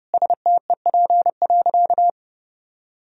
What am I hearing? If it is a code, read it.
Morse code, heard as a keyed tone, about 30 wpm, 700 Hz. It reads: STEP.
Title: Sentences from Top 1000 Words - Review Part 2 - 30wpm